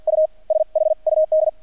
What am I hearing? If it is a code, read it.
Morse code, heard as a keyed tone, about 50 wpm, spaced as for 41 wpm, 630 Hz. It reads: U SHUD
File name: archived / CRSnet